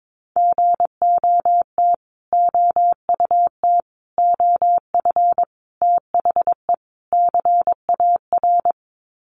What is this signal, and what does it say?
Morse code, heard as a keyed tone, about 22 wpm, 700 Hz. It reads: GOT OUT OF THE CAR
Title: Sets of 5 Words Encore 22wpm